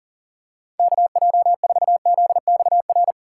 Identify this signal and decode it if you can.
Morse code, heard as a keyed tone, about 40 wpm, 700 Hz. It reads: KJ4ZXR
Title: USA Call Signs - 4x Character Spacing - 40wpm